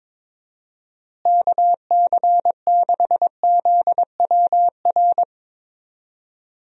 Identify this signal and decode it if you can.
Morse code, heard as a keyed tone, about 22 wpm, 700 Hz. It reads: KC6ZWR
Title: USA Call Signs - 6 Characters - Repeated 3x - 22wpm